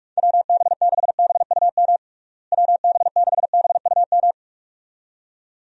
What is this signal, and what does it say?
Morse code, heard as a keyed tone, about 46 wpm, 700 Hz. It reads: WB6BUM WB6BUM
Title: USA Call Signs - 6 Characters - Repeated 2x - 45wpm